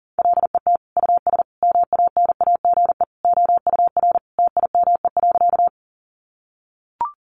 Morse code, heard as a keyed tone, about 40 wpm, 700 Hz. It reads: LET US MANAGE OUR TIME.
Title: Sentences from Top 1400 Words 40wpm